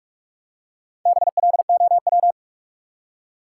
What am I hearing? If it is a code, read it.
Morse code, heard as a keyed tone, about 41 wpm, 700 Hz. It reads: BLOW
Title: Four Letter Words - 45wpm